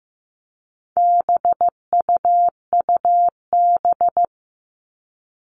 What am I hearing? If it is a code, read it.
Morse code, heard as a keyed tone, about 15 wpm, 700 Hz. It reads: BUUB
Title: All Cut Numbers - 4-Digit 15wpm